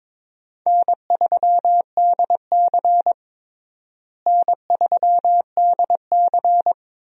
Morse code, heard as a keyed tone, about 22 wpm, 700 Hz. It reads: N3DC N3DC
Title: USA Call Signs - 4 Characters - Repeated 2x - 22wpm